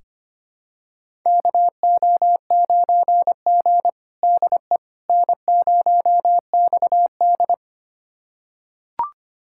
Morse code, heard as a keyed tone, about 25 wpm, 700 Hz. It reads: KO9G DE N0XD
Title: USA Callsign DE Callsign - 4 Characters - 25wpm